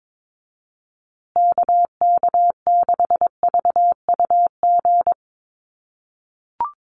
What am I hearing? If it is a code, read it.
Morse code, heard as a keyed tone, about 22 wpm, 700 Hz. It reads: KK6VUG
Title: USA Call Signs Repeated 3x 22wpm